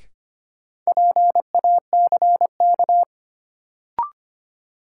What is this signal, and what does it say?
Morse code, heard as a keyed tone, about 25 wpm, 700 Hz. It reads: PACK
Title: Sentences from Top 1300 Words - New Word Drill - 25wpm